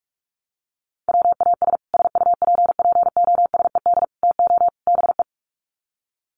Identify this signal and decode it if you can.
Morse code, heard as a keyed tone, about 44 wpm, 700 Hz. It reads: WAS SUPPOSED TO BE
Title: Sets of 4 Words 45wpm